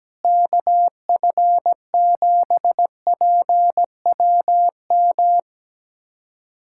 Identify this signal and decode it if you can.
Morse code, heard as a keyed tone, about 17 wpm, 700 Hz. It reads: KF7PWM
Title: USA Callsign DE Callsign 17wpm